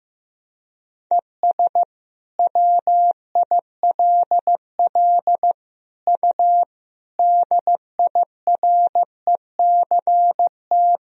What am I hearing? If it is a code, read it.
Morse code, heard as a keyed tone, about 16 wpm, 700 Hz. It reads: ES WILL U DIRECT
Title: Sentences from Top 400 Words and Abbreviations 15wpm